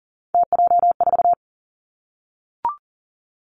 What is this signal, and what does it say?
Morse code, heard as a keyed tone, about 41 wpm, 700 Hz. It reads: TJ4
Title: Three Challenging Letter-Number Combinations 40wpm